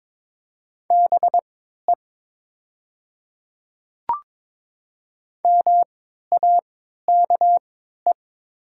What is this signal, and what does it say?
Morse code, heard as a keyed tone, about 22 wpm, spaced as for 12 wpm, 700 Hz. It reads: BE MAKE
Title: 3x Character Spacing - Top 300 Words - 22wpm